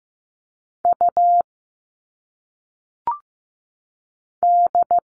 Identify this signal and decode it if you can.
Morse code, heard as a keyed tone, about 15 wpm, 700 Hz. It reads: U D
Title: All Cut Numbers - 1-Digit 15wpm